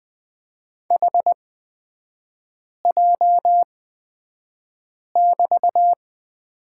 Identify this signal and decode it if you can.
Morse code, heard as a keyed tone, about 20 wpm, spaced as for 5 wpm, 700 Hz. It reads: HJ<BT>
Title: 243 Lesson 39e - Letter z - Three Characters 20wpm